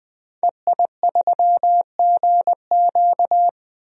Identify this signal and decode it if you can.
Morse code, heard as a keyed tone, about 20 wpm, 700 Hz. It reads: EI3GQ
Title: DXCC-100 Call Signs - 5 Characters - 20wpm